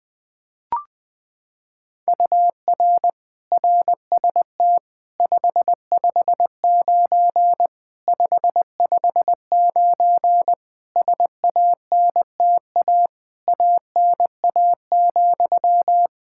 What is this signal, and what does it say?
Morse code, heard as a keyed tone, about 21 wpm, 700 Hz. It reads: UR RST 559 559 SANTA ANA,
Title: Practice QSO - Exchange 1 - Beginning - 20wpm